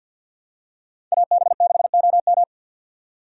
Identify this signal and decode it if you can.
Morse code, heard as a keyed tone, about 50 wpm, 700 Hz. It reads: AB6OM
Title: USA Call Signs - 5 Characters - 50wpm